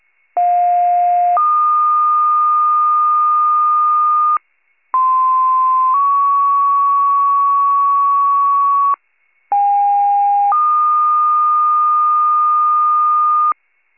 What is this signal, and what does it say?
Two-tone page, heard as a one-second tone, then a three-second tone, about 710 then 1220 Hz, about 1040 then 1120 Hz, about 810 then 1220 Hz. It is heard from 0.4 to 13.5 s.
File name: pager / 46 RESCUE / wcalert